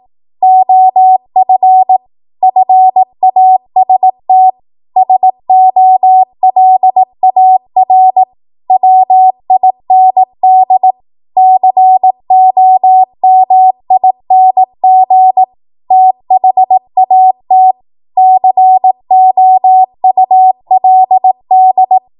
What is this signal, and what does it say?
Morse code, heard as a keyed tone, about 18 wpm, 750 Hz. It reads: OF FAST SOLAR WIND COMING THAT COULD
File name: Archive / 18 WPM / 18WPM